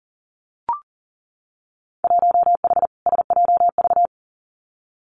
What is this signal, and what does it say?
Morse code, heard as a keyed tone, about 40 wpm, 700 Hz. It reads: 1H SJV
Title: Winter Field Day Exchanges 40wpm